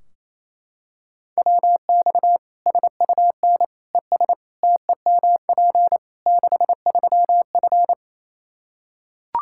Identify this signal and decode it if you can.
Morse code, heard as a keyed tone, about 27 wpm, 700 Hz. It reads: WX SUN ES TEMP 63F
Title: Practice QSO - Exchange 2 - Info - 28wpm